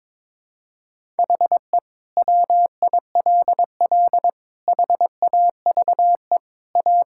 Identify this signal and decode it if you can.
Morse code, heard as a keyed tone, about 22 wpm, 700 Hz. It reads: HE WILL HAVE A
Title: Sentences from Top 1000 Words - Review Part 6 - 22wpm